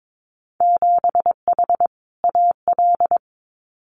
Morse code, heard as a keyed tone, about 22 wpm, 700 Hz. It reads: 7H AL
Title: Winter Field Day Exchanges Encore 22wpm